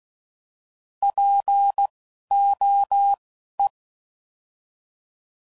Morse code, heard as a keyed tone, about 16 wpm, spaced as for 12 wpm, 800 Hz. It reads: POE
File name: Prefixes / By First Letter / Ps